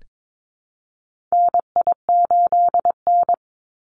Morse code, heard as a keyed tone, about 22 wpm, 700 Hz. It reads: NI8N
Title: Call Signs Repeated 2x - 22wpm